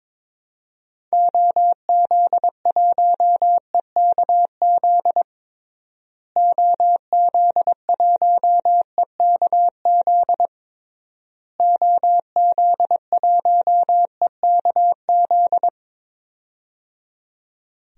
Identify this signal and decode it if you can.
Morse code, heard as a keyed tone, about 22 wpm, 700 Hz. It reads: OZ1EKZ OZ1EKZ OZ1EKZ
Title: DXCC-100 Call Signs - 6 Characters - Repeated 3x - 22wpm